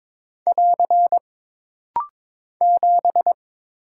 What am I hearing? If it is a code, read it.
Morse code, heard as a keyed tone, about 22 wpm, 700 Hz. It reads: <AR> 7
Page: This rapid-fire timing is geared to push your ICR (Instant Character Recognition) skills.